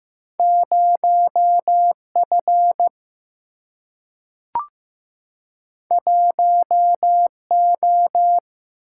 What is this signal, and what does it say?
Morse code, heard as a keyed tone, about 15 wpm, 700 Hz. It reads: 0 F 1 O
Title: Single-Letter Single-Number 15wpm